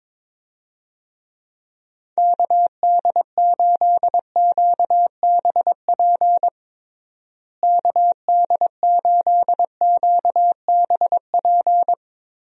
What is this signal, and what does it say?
Morse code, heard as a keyed tone, about 22 wpm, 700 Hz. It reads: KD8QBP KD8QBP
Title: USA Call Signs - 6 Characters - Repeated 3x - 22wpm